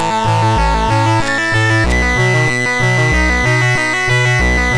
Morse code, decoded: none